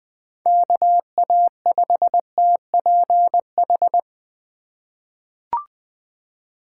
Morse code, heard as a keyed tone, about 20 wpm, 700 Hz. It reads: KA5TPH
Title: USA Call Signs - 6 Characters - 4x Character Spacing - 20wpm